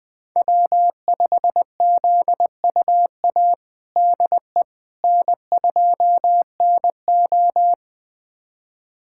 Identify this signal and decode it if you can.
Morse code, heard as a keyed tone, about 21 wpm, 700 Hz. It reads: W5ZUA DE N2NO
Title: USA Callsign DE Callsign 20wpm